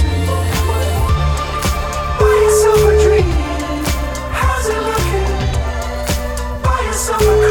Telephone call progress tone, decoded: ringback tone

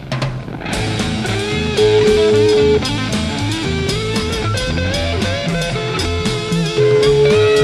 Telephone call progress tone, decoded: ringback tone

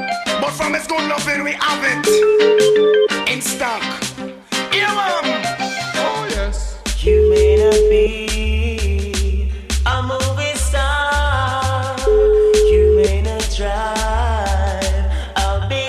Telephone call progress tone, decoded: ringback tone

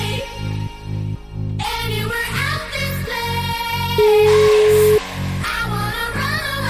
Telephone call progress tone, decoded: ringback tone

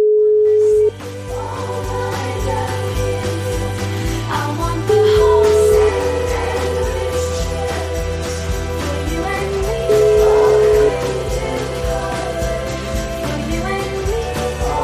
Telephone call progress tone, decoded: ringback tone

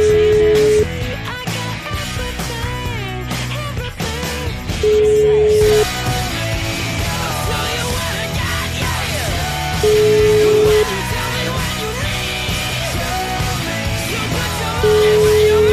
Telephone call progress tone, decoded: ringback tone

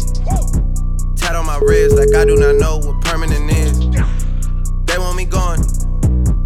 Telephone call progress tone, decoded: ringback tone

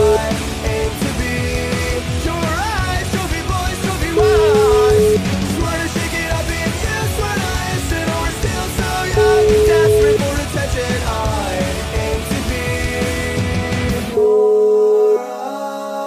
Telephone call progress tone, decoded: ringback tone

